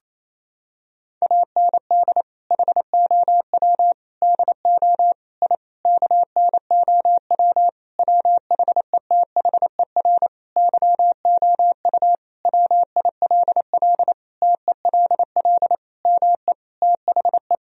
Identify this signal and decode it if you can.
Morse code, heard as a keyed tone, about 28 wpm, 700 Hz. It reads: AND HOW DO I KNOW WHETHER YOU WILL TELL ME THE
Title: Sentences from Top 1000 Words - Review Part 3 - 28wpm